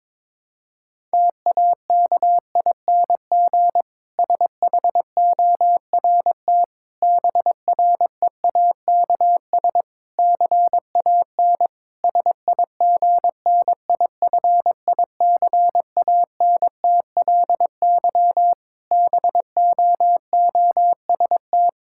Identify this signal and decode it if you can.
Morse code, heard as a keyed tone, about 22 wpm, 700 Hz. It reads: TAKING SHORT BREAKS CAN SIGNIFICANTLY BOOST
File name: Difficult Sentences with Hints - 22wpm